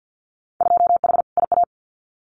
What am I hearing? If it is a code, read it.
Morse code, heard as a keyed tone, about 50 wpm, 700 Hz. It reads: 2H IA
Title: Winter Field Day Exchanges Encore 50wpm